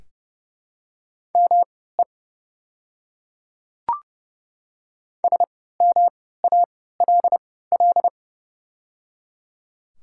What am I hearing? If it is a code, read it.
Morse code, heard as a keyed tone, about 30 wpm, spaced as for 17 wpm, 700 Hz. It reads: ME SMALL
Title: 3x Character Spacing - Top 300 Words - 30wpm